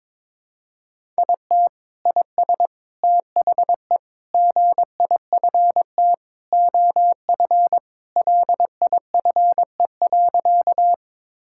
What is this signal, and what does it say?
Morse code, heard as a keyed tone, about 22 wpm, 700 Hz. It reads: IT IS THE GIFT OF LIFE.
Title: Sentences from Top 1000 Words - Review Part 6 - 22wpm